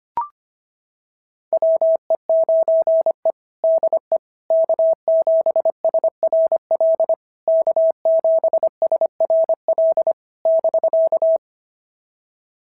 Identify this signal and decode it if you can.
Morse code, heard as a keyed tone, about 24 wpm, 650 Hz. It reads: WE9E DE K7SRL K7SRL <BK>
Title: SOTA QSO Practice 25wpm